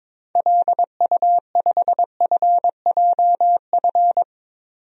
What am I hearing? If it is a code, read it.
Morse code, heard as a keyed tone, about 22 wpm, 700 Hz. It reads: LU5FJF
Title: DXCC-100 Call Signs - 6 Characters - 22wpm